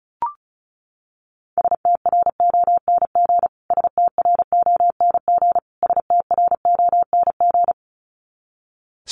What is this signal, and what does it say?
Morse code, heard as a keyed tone, about 34 wpm, 700 Hz. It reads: STRONG STRONG STRONG
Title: Top 500 Words Repeated 35wpm